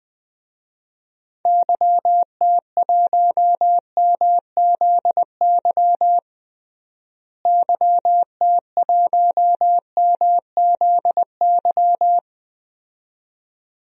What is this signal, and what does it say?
Morse code, heard as a keyed tone, about 20 wpm, 700 Hz. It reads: YT1MZY YT1MZY
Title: DXCC-100 Call Signs - 6 Characters - Repeated 2x - 20wpm